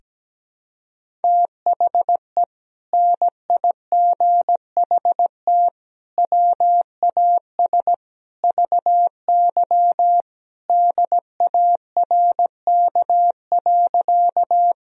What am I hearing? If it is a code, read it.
Morse code, heard as a keyed tone, about 17 wpm, 700 Hz. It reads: THE NIGHT WAS VY DARK.
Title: Sentences from Top 500 Words and Abbreviations 17wpm